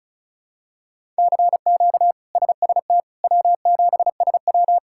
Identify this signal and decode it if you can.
Morse code, heard as a keyed tone, about 36 wpm, 700 Hz. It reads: CQ SST W7SW
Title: K1USN SST QSO Practice - 35wpm